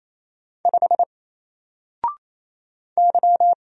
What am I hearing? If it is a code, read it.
Morse code, heard as a keyed tone, about 27 wpm, 700 Hz. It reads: 5 Y
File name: Single-Letter-Number-28wpm